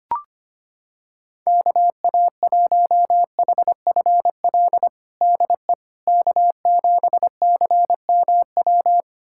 Morse code, heard as a keyed tone, about 25 wpm, 700 Hz. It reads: KA1HFL DE K7CMW
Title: USA Callsign DE Callsign 25wpm